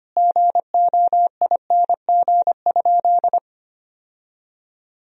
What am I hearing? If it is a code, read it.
Morse code, heard as a keyed tone, about 25 wpm, 700 Hz. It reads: GOING?
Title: Sentences from Top 1000 Words - Review Part 3 - 25wpm